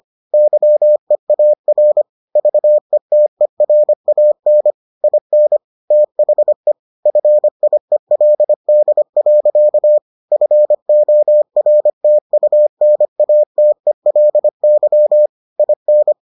Morse code, heard as a keyed tone, about 25 wpm, 600 Hz. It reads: YEAR VETERAN IN THE FIELD. FORTUNATELY IN